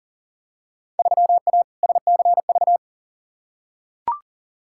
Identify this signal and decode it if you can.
Morse code, heard as a keyed tone, about 40 wpm, 700 Hz. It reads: 3A SCV
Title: Field Day Exchanges - 40wpm